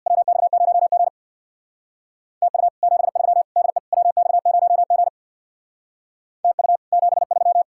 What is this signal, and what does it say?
Morse code, heard as a keyed tone, about 57 wpm, 700 Hz. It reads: WX0G TU 73 DE WX0G TU 73
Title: Speed-Racing Common QSO Phrases - 50wpm